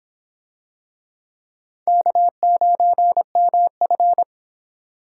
Morse code, heard as a keyed tone, about 24 wpm, 700 Hz. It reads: K9MF